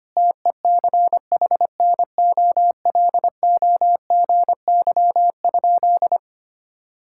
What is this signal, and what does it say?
Morse code, heard as a keyed tone, about 25 wpm, 700 Hz. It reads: TECHNOLOGY?
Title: Sentences from Top 1000 Words - Review Part 5 - 25wpm